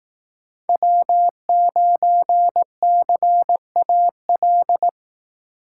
Morse code, heard as a keyed tone, about 18 wpm, 700 Hz. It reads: W9CAL
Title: Speed Racing - USA Call Signs 15wpm